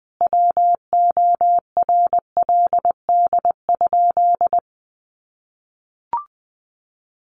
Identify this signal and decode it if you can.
Morse code, heard as a keyed tone, about 20 wpm, 700 Hz. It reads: WORLD?
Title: Sentences from Top 1400 Words 20wpm